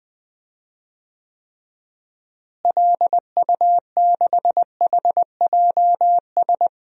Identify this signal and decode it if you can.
Morse code, heard as a keyed tone, about 20 wpm, 700 Hz. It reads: LU6HJS
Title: DXCC-100 Call Signs - 6 Characters - Repeated 3x - 20wpm